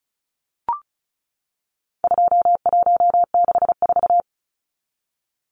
Morse code, heard as a keyed tone, about 34 wpm, 700 Hz. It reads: 2164
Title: Numbers - 4-Digit 35wpm